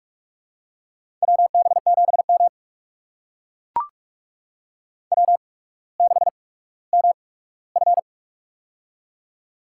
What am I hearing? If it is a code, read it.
Morse code, heard as a keyed tone, about 44 wpm, spaced as for 19 wpm, 690 Hz. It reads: WB7M W6MF
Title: USA Call Signs - 4 Characters - 8x Character Spacing - 45wpm